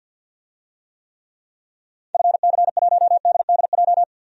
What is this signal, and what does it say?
Morse code, heard as a keyed tone, about 47 wpm, 700 Hz. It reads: FC1DDJ
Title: DXCC-100 Call Signs - 6 Characters - 50wpm